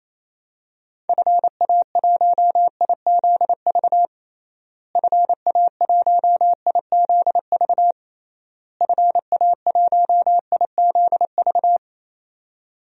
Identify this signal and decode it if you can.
Morse code, heard as a keyed tone, about 28 wpm, 700 Hz. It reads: FA1IZV FA1IZV FA1IZV
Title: DXCC-100 Call Signs - 6 Characters - Repeated 3x - 28wpm